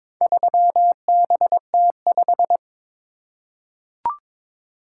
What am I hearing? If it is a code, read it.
Morse code, heard as a keyed tone, about 22 wpm, 700 Hz. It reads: 3BT5
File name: practice / 22wpm / Four Letter-Number Combinations - 22wpm